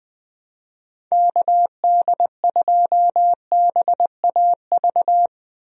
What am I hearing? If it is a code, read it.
Morse code, heard as a keyed tone, about 20 wpm, 700 Hz. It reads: KD2BAV
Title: USA Call Signs - 6 Characters - Repeated 3x - 20wpm